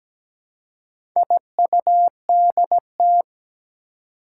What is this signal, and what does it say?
Morse code, heard as a keyed tone, about 17 wpm, 700 Hz. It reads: IUDT
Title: Four Challenging Letter-Number Combinations 17wpm